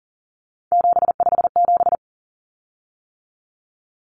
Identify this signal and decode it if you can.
Morse code, heard as a keyed tone, about 40 wpm, 700 Hz. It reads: 757
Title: Common Cut Numbers - 3-Digit 40wpm